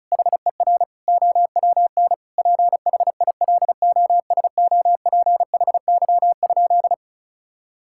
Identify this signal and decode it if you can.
Morse code, heard as a keyed tone, about 36 wpm, 700 Hz. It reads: HER OWN PHILOSOPHY?
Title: Sentences from Top 1000 Words - Review Part 6 - 35wpm